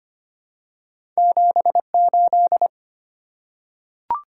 Morse code, heard as a keyed tone, about 25 wpm, 700 Hz. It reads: 78
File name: Two Letter-Number Combinations - 25wpm